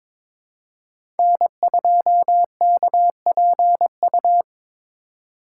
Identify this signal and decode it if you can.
Morse code, heard as a keyed tone, about 21 wpm, 700 Hz. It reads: N2KPU
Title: USA Call Signs - 3x Character Spacing - 22wpm